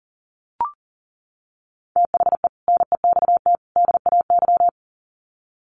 Morse code, heard as a keyed tone, about 40 wpm, 700 Hz. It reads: THE NEXT DAY
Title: Sets of 3 Words 40wpm